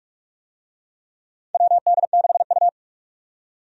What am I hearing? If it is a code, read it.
Morse code, heard as a keyed tone, about 46 wpm, 700 Hz. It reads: WD6U
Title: USA Call Signs - 4 Characters - 45wpm